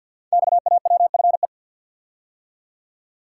Morse code, heard as a keyed tone, about 50 wpm, 700 Hz. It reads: XAWFE
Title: Code Groups - 5 Letters - 50wpm